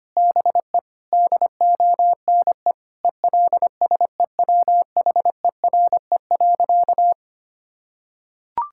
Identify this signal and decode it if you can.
Morse code, heard as a keyed tone, about 25 wpm, 700 Hz. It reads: BE DONE ELSEWHERE.
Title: Sentences from Top 1600 Words 25wpm